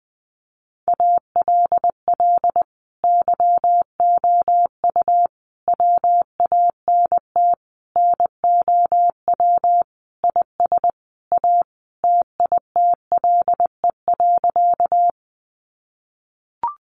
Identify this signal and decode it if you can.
Morse code, heard as a keyed tone, about 20 wpm, 700 Hz. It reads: ALL YOU WANT NOW IS A TITLE.
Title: Sentences from Top 1200 Words 20wpm